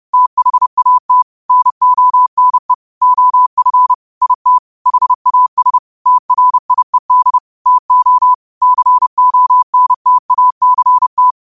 Morse code, heard as a keyed tone, about 30 wpm, 1000 Hz. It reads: THAT NONE OF IT HAS TRIED TO CONTACT